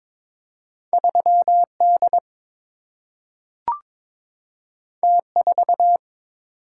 Morse code, heard as a keyed tone, about 22 wpm, 700 Hz. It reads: 3 D T 4
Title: Single-Letter Single-Number 22wpm